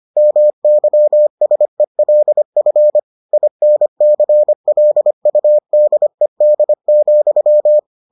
Morse code, heard as a keyed tone, about 25 wpm, 600 Hz. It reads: MYSELF INCLUDED,